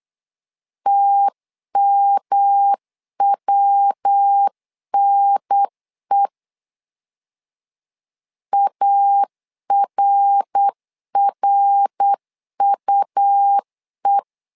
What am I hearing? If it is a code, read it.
Morse code, heard as a keyed tone, about 8 wpm, 800 Hz. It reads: TMWNE ARRUE
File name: LEZIONI / 3 MWRO / 3 MWR 8wpm 2444